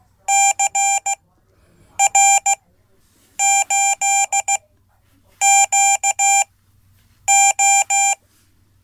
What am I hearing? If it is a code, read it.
Morse code, heard as a keyed tone, about 16 wpm, spaced as for 8 wpm, 790 Hz. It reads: CR8QO
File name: Code / 5min-b